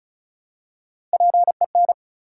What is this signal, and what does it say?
Morse code, heard as a keyed tone, about 36 wpm, 700 Hz. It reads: PEN
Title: Three Letter Words - 35wpm